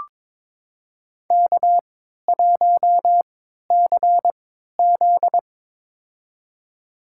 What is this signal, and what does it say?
Morse code, heard as a keyed tone, about 22 wpm, spaced as for 13 wpm, 700 Hz. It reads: K1CZ
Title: USA Call Signs - 4 Characters - 3x Character Spacing - 22wpm